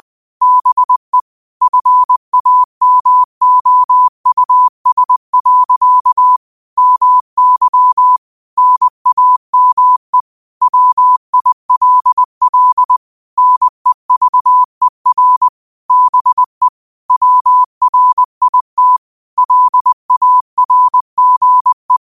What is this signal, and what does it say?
Morse code, heard as a keyed tone, about 20 wpm, 1000 Hz. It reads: BE FAMOUS. MY NAME WILL NEVER BE WRIT LARGE